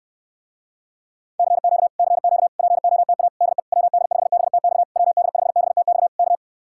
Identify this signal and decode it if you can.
Morse code, heard as a keyed tone, about 64 wpm, 700 Hz. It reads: CQ CQ POTA DE WD5BTX WD5BTX K